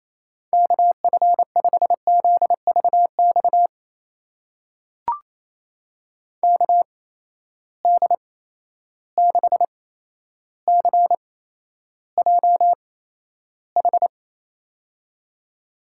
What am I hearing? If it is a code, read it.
Morse code, heard as a keyed tone, about 28 wpm, spaced as for 12 wpm, 700 Hz. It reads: KF5ZVX KD6CJH
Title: USA Call Signs - 6 Characters - 8x Character Spacing - 28wpm